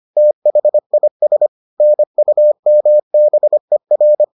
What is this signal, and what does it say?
Morse code, heard as a keyed tone, about 25 wpm, 600 Hz. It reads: THIS NUMBER